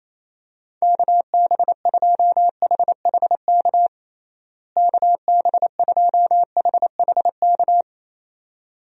Morse code, heard as a keyed tone, about 27 wpm, 700 Hz. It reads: KB2HHK KB2HHK
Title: USA Call Signs - 6 Characters - Repeated 2x - 28wpm